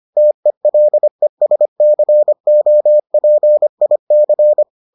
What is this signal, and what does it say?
Morse code, heard as a keyed tone, about 25 wpm, 600 Hz. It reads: TELESCOPIC